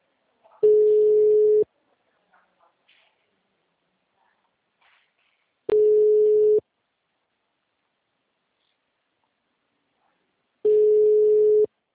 Ringback tone (Europe, 425 Hz).